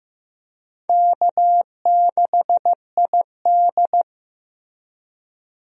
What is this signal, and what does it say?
Morse code, heard as a keyed tone, about 16 wpm, 700 Hz. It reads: K6ID
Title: USA Call Signs - 4 Characters - Repeated 2x - 15wpm